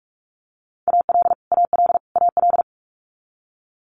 Morse code, heard as a keyed tone, about 46 wpm, 700 Hz. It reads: AL AL AL
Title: Multi-letter Phonemes Repeated 3x 45wpm